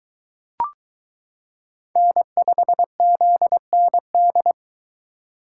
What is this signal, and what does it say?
Morse code, heard as a keyed tone, about 23 wpm, 700 Hz. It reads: N5ZND